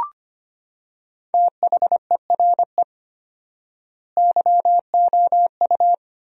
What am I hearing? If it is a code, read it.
Morse code, heard as a keyed tone, about 25 wpm, 700 Hz. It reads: THERE YOU
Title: 4x Word-Spacing - Sentences from Top 100 Words - 25wpm